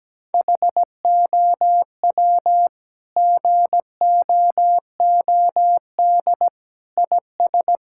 Morse code, heard as a keyed tone, about 17 wpm, 700 Hz. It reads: HOW GOOD IS
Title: Sentences from Top 1000 Words - Review Part 6 - 17wpm